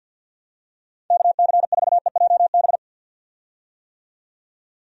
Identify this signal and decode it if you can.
Morse code, heard as a keyed tone, about 50 wpm, 700 Hz. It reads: KC4EJB
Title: USA Call Signs - 6 Characters - 50wpm